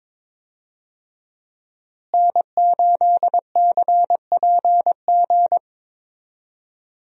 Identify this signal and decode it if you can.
Morse code, heard as a keyed tone, about 22 wpm, 700 Hz. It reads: N8CPG
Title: USA Call Signs - 5 Characters - Repeated 3x - 22wpm